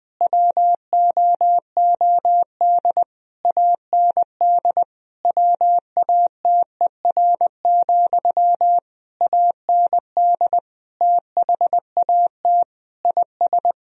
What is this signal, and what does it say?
Morse code, heard as a keyed tone, about 21 wpm, 700 Hz. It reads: WOOD AND WATER, AND THAT IS
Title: Sentences from Top 1000 Words - Review Part 2 - 20wpm